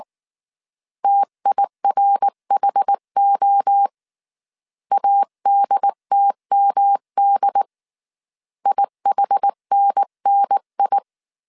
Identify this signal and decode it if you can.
Morse code, heard as a keyed tone, about 19 wpm, 800 Hz. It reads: TIRHO ADTMD IHNNI
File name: CW / DET / 1 3 1500C S B 18wpm 1916